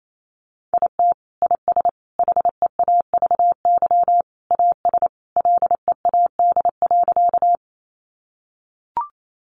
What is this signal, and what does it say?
Morse code, heard as a keyed tone, about 28 wpm, 700 Hz. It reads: IT IS HEAVY AS LEAD.
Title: Sentences from Top 1100 Words 28wpm